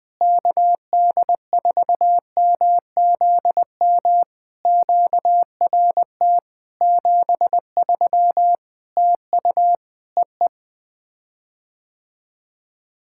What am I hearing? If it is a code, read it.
Morse code, heard as a keyed tone, about 20 wpm, 700 Hz. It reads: KD4MZM QRT 73 TU EE
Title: POTA QSO Practice - 20wpm